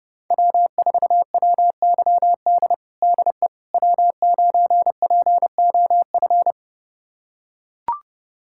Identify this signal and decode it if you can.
Morse code, heard as a keyed tone, about 32 wpm, 700 Hz. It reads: W4WYD DE W9POF
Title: USA Callsign DE Callsign - 5 Characters - 30wpm